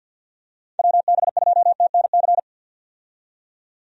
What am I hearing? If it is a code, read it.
Morse code, heard as a keyed tone, about 51 wpm, 700 Hz. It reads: WB2TNC
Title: USA Call Signs - 6 Characters - 50wpm